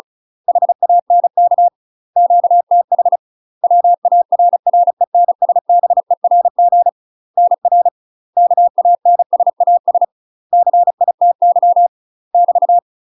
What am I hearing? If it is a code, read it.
Morse code, heard as a keyed tone, about 35 wpm, 700 Hz. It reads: HANK QTH WARRENSBERG NR KANSAS CITY <BT>
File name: UK MASTER QSOs TRAINING / 35WPM / UK MASTER QSOs 0014